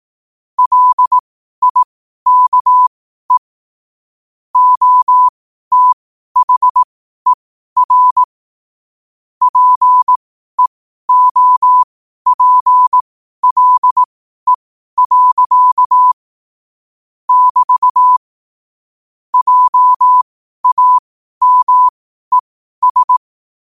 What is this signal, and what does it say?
Morse code, heard as a keyed tone, about 18 wpm, spaced as for 12 wpm, 1000 Hz. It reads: LIKE OTHER PEOPLE. <BT> JAMES
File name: qotdfiles / 12wpm